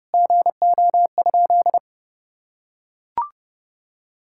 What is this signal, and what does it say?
Morse code, heard as a keyed tone, about 32 wpm, 700 Hz. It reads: GO?
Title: Sentences from Top 1600 Words 30wpm